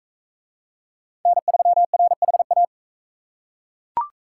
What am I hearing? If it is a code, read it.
Morse code, heard as a keyed tone, about 37 wpm, 700 Hz. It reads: N3RHA